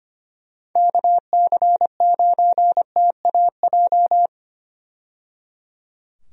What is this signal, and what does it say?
Morse code, heard as a keyed tone, about 25 wpm, 700 Hz. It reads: KC9TAJ
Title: USA Call Signs - 6 Characters - 25wpm